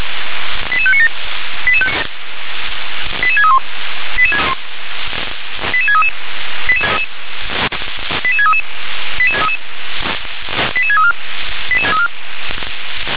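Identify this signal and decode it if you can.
Five-tone selective call (ZVEI), heard as quick runs of five tones, tones of about 70 ms. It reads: 88597 88631 88633 88633 88634
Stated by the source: ZVEI five-tone selective call received on SDR.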